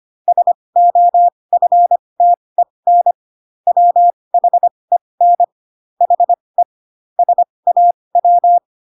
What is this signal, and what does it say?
Morse code, heard as a keyed tone, about 25 wpm, spaced as for 20 wpm, 700 Hz. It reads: SOFTEN WHEN HE SAW
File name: Pinocchio / Pinocchio ch8 15 25 ik1hge